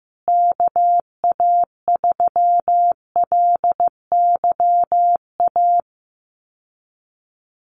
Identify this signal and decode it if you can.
Morse code, heard as a keyed tone, about 14 wpm, 700 Hz. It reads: KA3LYA
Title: USA Call Signs Repeated 3x 15wpm